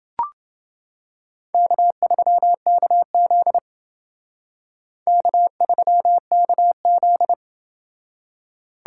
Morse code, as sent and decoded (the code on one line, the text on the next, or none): -.- ...-- -.- --.. / -.- ...-- -.- --..
K3KZ K3KZ